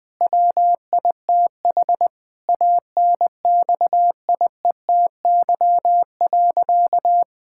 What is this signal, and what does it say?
Morse code, heard as a keyed tone, about 20 wpm, 700 Hz. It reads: WITH ANXIETY.
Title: Sentences from Top 1600 Words 20wpm